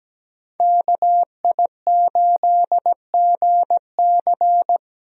Morse code, heard as a keyed tone, about 17 wpm, 700 Hz. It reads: KI8GC
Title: USA Call Signs - 5 Characters - Repeated 3x - 17wpm